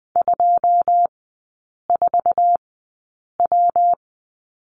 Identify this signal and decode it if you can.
Morse code, heard as a keyed tone, about 20 wpm, spaced as for 8 wpm, 690 Hz. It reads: 24W
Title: 133 Lesson 22e - Question Mark - Three Characters 20wpm